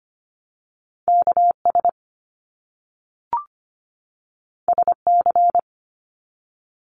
Morse code, heard as a keyed tone, about 25 wpm, 700 Hz. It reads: KS SC
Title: US States 25wpm